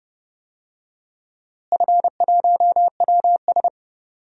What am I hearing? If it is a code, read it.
Morse code, heard as a keyed tone, about 30 wpm, 700 Hz. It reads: F1WS
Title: DXCC-100 Call Signs - 4 Characters - Repeated 3x - 30wpm